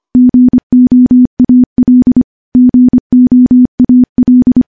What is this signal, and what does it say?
Morse code, heard as a keyed tone, about 25 wpm, 250 Hz. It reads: GOAL GOAL
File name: morse-training-25WPM